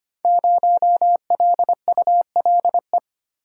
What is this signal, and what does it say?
Morse code, heard as a keyed tone, about 25 wpm, 700 Hz. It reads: 0LULE
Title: Code Groups - 5 Letter:Numbers - 25wpm